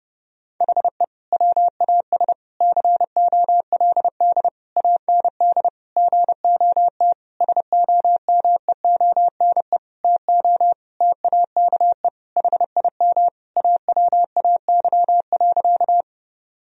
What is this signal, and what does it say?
Morse code, heard as a keyed tone, about 32 wpm, 700 Hz. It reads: HE WAS COLD AND GOT SOMEONE TO TAKE HIM AWAY.
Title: Sentences from Top 1000 Words - Review Part 4 - 30wpm